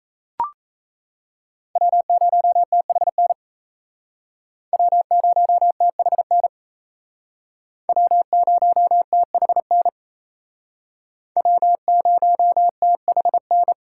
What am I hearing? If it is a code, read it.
Morse code, heard as a keyed tone, about 37 wpm, 700 Hz. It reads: W0THN W0THN W0THN W0THN